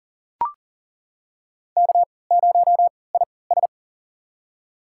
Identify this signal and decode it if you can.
Morse code, heard as a keyed tone, about 40 wpm, spaced as for 23 wpm, 700 Hz. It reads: K0IS
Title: USA Call Signs - 4 Characters - 3x Character Spacing - 40wpm